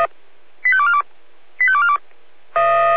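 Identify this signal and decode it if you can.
Five-tone selective call (ZVEI), heard as quick runs of five tones, tones of about 70 ms. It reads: 85242 85242
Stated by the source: ZVEI five-tone call 85220 received by SDR, from the server of Feuerwehr Orlamünde (fire brigade).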